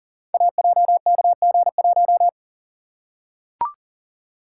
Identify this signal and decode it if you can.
Morse code, heard as a keyed tone, about 43 wpm, 700 Hz. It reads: AJKG1
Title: Code Groups - 5 Letter:Numbers - 40wpm